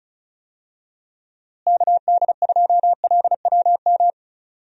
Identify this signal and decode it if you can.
Morse code, heard as a keyed tone, about 36 wpm, 700 Hz. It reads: KD2LWM
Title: USA Call Signs - 6 Characters - Repeated 3x - 35wpm